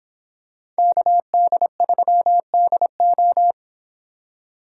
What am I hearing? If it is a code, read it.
Morse code, heard as a keyed tone, about 26 wpm, 700 Hz. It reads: KD3DO